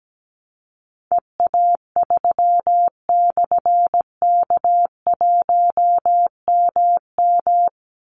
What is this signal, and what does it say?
Morse code, heard as a keyed tone, about 17 wpm, 700 Hz. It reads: EA3/K1MM
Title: Call Signs - 17wpm